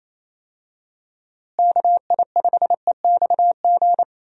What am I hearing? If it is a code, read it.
Morse code, heard as a keyed tone, about 28 wpm, 700 Hz. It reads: KI5EXG
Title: USA Call Signs - 6 Characters - Repeated 2x - 28wpm